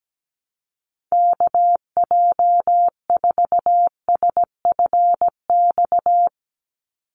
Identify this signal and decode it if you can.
Morse code, heard as a keyed tone, about 17 wpm, 700 Hz. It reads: KJ4SFX
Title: USA Call Signs 17wpm